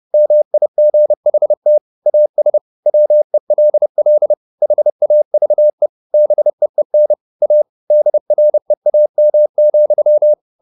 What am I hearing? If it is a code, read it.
Morse code, heard as a keyed tone, about 30 wpm, 600 Hz. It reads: MIGHT AS WELL HAVE BEEN A DREAM,